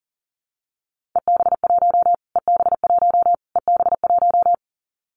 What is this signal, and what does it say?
Morse code, heard as a keyed tone, about 40 wpm, 700 Hz. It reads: EB1 EB1 EB1
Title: Callsign Prefixes for Top 25 DXCC - Repeated 3x - 40wpm